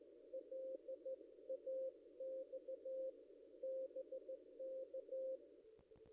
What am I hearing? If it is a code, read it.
Morse code, heard as a keyed tone, about 14 wpm, 530 Hz. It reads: LAX BK
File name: contests / ss-cw-k5tr-op-n5rz -1600